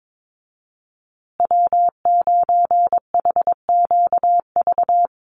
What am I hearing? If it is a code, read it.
Morse code, heard as a keyed tone, about 22 wpm, 700 Hz. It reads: W9HQV
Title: USA Call Signs Repeated 2x 22wpm